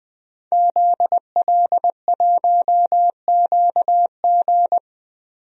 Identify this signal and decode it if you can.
Morse code, heard as a keyed tone, about 20 wpm, 700 Hz. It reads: ZL1QG
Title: DXCC-100 Call Signs - 5 Characters - Repeated 2x - 20wpm